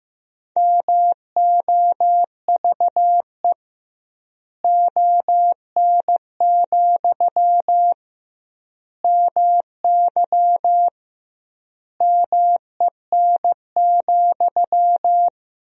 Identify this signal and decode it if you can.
Morse code, heard as a keyed tone, about 15 wpm, 700 Hz. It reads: MOVE ON, MY MEN,
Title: 2x Word-Spacing - Sentences from Top 500 Words - 15wpm